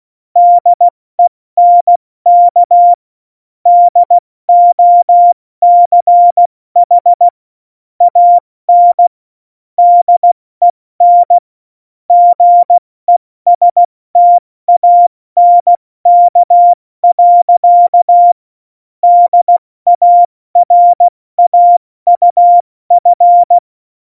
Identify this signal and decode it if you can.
Morse code, heard as a keyed tone, about 16 wpm, 700 Hz. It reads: DENK DOCH AN DEN GESTANK. DARAUF
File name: Morsekurs / Witze / Witz 21. Januar 2020, 70 F80